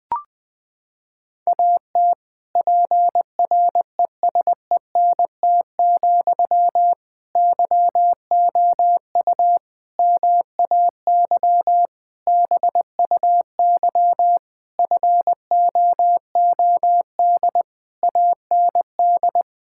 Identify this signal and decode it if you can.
Morse code, heard as a keyed tone, about 21 wpm, 700 Hz. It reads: AT PRESENT, YOU MAY BUY FOOD AND
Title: Sentences from Top 1000 Words - Review Part 5 - 20wpm